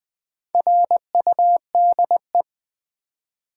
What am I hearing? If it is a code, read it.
Morse code, heard as a keyed tone, about 20 wpm, 700 Hz. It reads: RUDE
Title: Build A Word - 1 Letter At A Time - 4-Letter Words - 20wpm